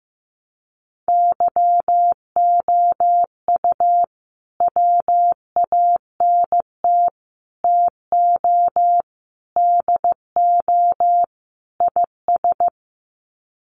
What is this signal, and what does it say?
Morse code, heard as a keyed tone, about 15 wpm, 700 Hz. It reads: YOU WANT TO DO IS
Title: Sets of 5 Words Encore2 15wpm